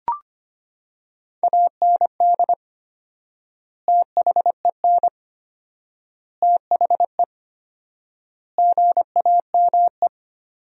Morse code, heard as a keyed tone, about 25 wpm, 700 Hz. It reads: AND THEN THE GAME
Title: 4x Word-Spacing - Sentences from Top 500 Words - 25wpm